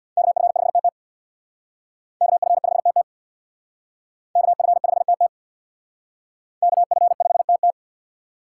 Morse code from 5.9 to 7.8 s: KF5TT

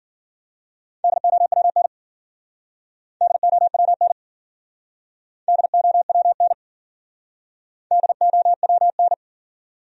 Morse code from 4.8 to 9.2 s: DOWN DOWN

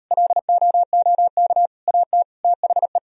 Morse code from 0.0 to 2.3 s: LOOK AT

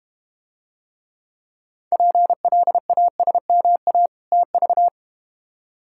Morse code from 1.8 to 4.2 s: PLASMA

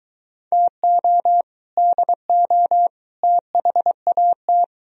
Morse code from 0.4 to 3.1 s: TO DO

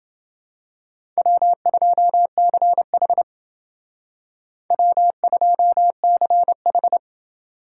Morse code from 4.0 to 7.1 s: W2CH

